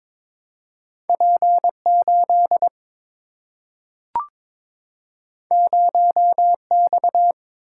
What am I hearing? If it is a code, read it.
Morse code, heard as a keyed tone, about 22 wpm, 700 Hz. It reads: P 8 0 X
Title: Single-Letter Single-Number 22wpm